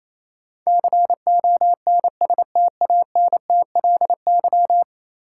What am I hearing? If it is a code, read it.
Morse code, heard as a keyed tone, about 28 wpm, 700 Hz. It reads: CONSTANTLY
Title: Sentences from Top 1000 Words - Review Part 7 - New Word Drill - 28wpm